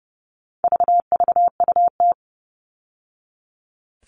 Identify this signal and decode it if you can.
Morse code, heard as a keyed tone, about 30 wpm, 700 Hz. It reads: VVUT
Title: All Cut Numbers - 4-Digit 30wpm